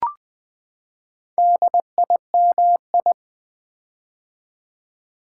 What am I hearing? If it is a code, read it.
Morse code, heard as a keyed tone, about 20 wpm, 700 Hz. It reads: DIMI
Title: Build A Word - 4 Letters At A Time - 8-Letter Words - 20wpm